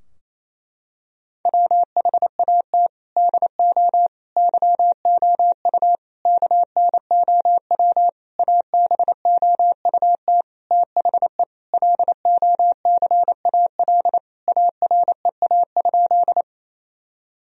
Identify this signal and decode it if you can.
Morse code, heard as a keyed tone, about 28 wpm, 700 Hz. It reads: WHAT DO YOU KNOW ABOUT THE LOCAL AREA?
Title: Sentences from Top 1000 Words - Review Part 3 - 28wpm